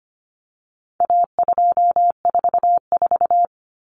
Morse code, heard as a keyed tone, about 25 wpm, 700 Hz. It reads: A244
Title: Common Cut Numbers - 4-Digit 25wpm